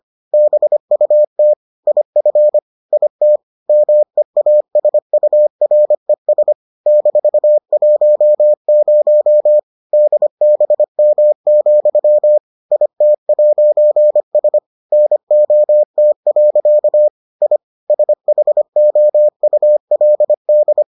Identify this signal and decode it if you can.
Morse code, heard as a keyed tone, about 25 wpm, 600 Hz. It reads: BUT IF IT MEASURES -10 DBM, IT'S NOT. I SHOULD